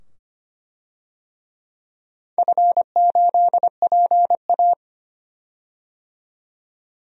Morse code, heard as a keyed tone, about 25 wpm, 700 Hz. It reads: F8PA
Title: DXCC-100 Call Signs - 4 Characters - 25wpm